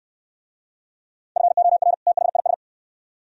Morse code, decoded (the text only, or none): FOR THIS